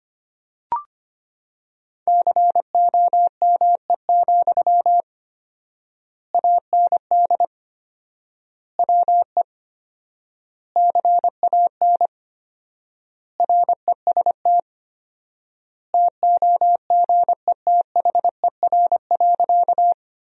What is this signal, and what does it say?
Morse code, heard as a keyed tone, about 25 wpm, 700 Hz. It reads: COME, AND WE CAN REST TOGETHER.
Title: 4x Word-Spacing - Sentences from Top 500 Words - 25wpm